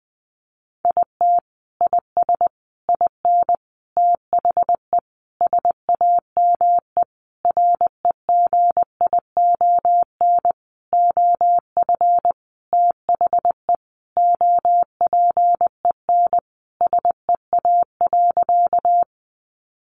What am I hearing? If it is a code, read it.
Morse code, heard as a keyed tone, about 20 wpm, 700 Hz. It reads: IT IS IN THE SAME REGION OF THE OPEN SEA.
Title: Sentences from Top 1500 Words 20wpm